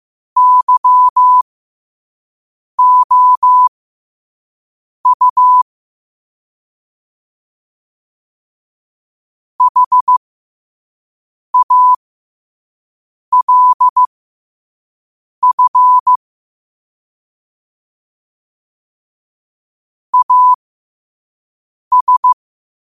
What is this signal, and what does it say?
Morse code, heard as a keyed tone, about 15 wpm, spaced as for 5 wpm, 1000 Hz. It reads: YOU HALF AS
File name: qotdfiles / 5wpm